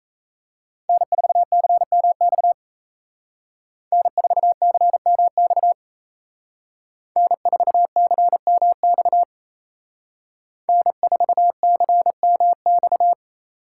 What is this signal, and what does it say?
Morse code, heard as a keyed tone, about 33 wpm, 700 Hz. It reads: N4CMX N4CMX N4CMX N4CMX
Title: Speed Racing - USA 1x3 Call Signs 28wpm